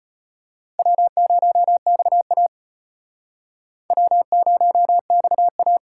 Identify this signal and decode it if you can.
Morse code, heard as a keyed tone, about 32 wpm, 700 Hz. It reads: W0XA W0XA